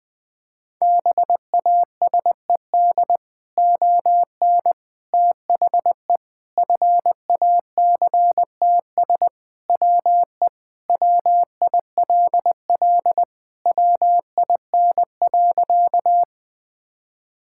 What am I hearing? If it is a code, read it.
Morse code, heard as a keyed tone, about 20 wpm, 700 Hz. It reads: BASED ON THE FACTS WE WILL WIN.
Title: Sentences with 3-Word Phrases - 20wpm